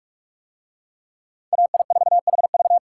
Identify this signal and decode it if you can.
Morse code, heard as a keyed tone, about 45 wpm, 700 Hz. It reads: AI4HV